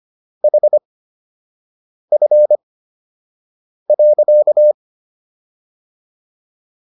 Morse code, heard as a keyed tone, about 25 wpm, spaced as for 6 wpm, 600 Hz. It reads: HF.